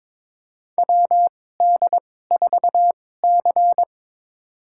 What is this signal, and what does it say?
Morse code, heard as a keyed tone, about 22 wpm, spaced as for 16 wpm, 700 Hz. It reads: WD4C
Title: USA Call Signs - 2x Character Spacing - 22wpm